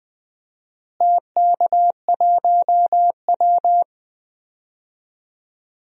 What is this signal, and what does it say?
Morse code, heard as a keyed tone, about 20 wpm, 700 Hz. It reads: TK1W
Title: DXCC-100 Call Signs - 4 Characters - Repeated 2x - 20wpm